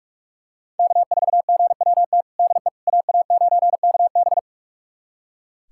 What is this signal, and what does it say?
Morse code, heard as a keyed tone, about 45 wpm, 700 Hz. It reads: K4GWT DE AA9KB
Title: USA Callsign DE Callsign 45wpm